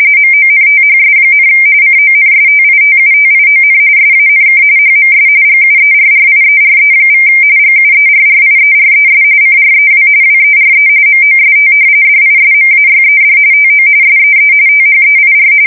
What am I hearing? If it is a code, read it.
RTTY, about 45.45 baud. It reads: MONTHS, CONSTITUTION SANK HMS GUERRIERE= GUERRIERES CREW WERE MOST DISMAYED TO SEE THEIR
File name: fsk170 / usnavy2